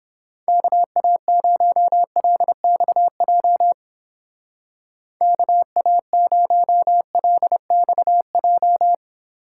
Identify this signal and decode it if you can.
Morse code, heard as a keyed tone, about 30 wpm, 700 Hz. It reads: KA0LXJ KA0LXJ